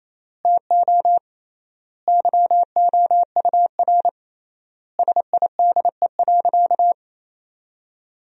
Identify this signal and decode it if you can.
Morse code, heard as a keyed tone, about 28 wpm, 700 Hz. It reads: TO YOUR SIDE.
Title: 3x Word-Spacing - Sentences from Top 400 Words - 28wpm